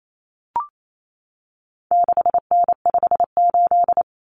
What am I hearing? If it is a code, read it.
Morse code, heard as a keyed tone, about 28 wpm, 700 Hz. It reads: 6N58
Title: Common Cut Numbers - 4-Digit 28wpm